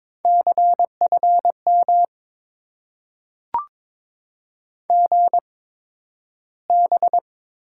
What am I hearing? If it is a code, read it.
Morse code, heard as a keyed tone, about 22 wpm, spaced as for 12 wpm, 700 Hz. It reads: CFM GB
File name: Top 90 QSO Elements - 8x Character Spacing - 22wpm